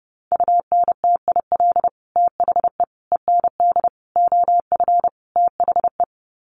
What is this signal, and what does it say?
Morse code, heard as a keyed tone, about 28 wpm, 700 Hz. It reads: UNTIL THE END OF THE
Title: Sets of 5 Words Encore 30wpm